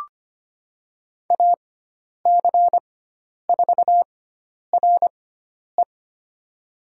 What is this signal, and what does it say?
Morse code, heard as a keyed tone, about 25 wpm, spaced as for 10 wpm, 700 Hz. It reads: AC4RE
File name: Lesson 16g - Letter c - Call Signs 25wpm